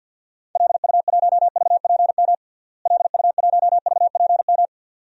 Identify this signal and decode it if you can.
Morse code, heard as a keyed tone, about 52 wpm, 700 Hz. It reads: LU1VPM LU1VPM
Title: DXCC-100 Call Signs - 6 Characters - Repeated 2x - 50wpm